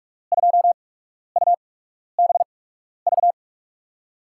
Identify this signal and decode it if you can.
Morse code, heard as a keyed tone, about 47 wpm, spaced as for 12 wpm, 700 Hz. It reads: 2UBV
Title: USA Call Signs - 5 Characters - 8x Character Spacing - 45wpm